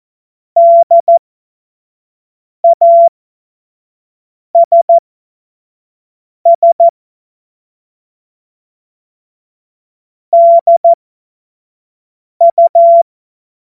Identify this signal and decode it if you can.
Morse code, heard as a keyed tone, about 14 wpm, spaced as for 5 wpm, 680 Hz. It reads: DASS DU
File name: Morsekurs / Witze / Witz 10. Mai 2022 25 F70